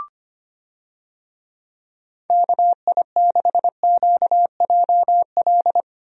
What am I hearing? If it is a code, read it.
Morse code, heard as a keyed tone, about 25 wpm, 700 Hz. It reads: KI6QJL
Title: USA Call Signs - 6 Characters - Repeated 2x - 25wpm